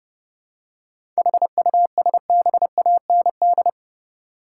Morse code, heard as a keyed tone, about 30 wpm, 700 Hz. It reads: HUSBAND
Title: Seven Letter Words - 30wpm